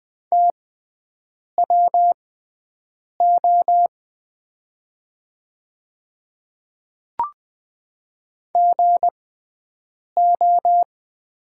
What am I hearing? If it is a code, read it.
Morse code, heard as a keyed tone, about 20 wpm, spaced as for 7 wpm, 690 Hz. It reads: TWO GO
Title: 6x Character Spacing - Top 100 Words - 20wpm